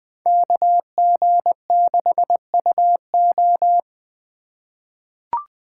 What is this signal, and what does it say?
Morse code, heard as a keyed tone, about 20 wpm, 700 Hz. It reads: KG6UO
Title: USA Call Signs - 5 Characters - 2x Character Spacing - 20wpm